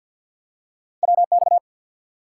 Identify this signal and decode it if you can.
Morse code, heard as a keyed tone, about 53 wpm, 700 Hz. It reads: WX
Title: Top 90 QSO Elements 50wpm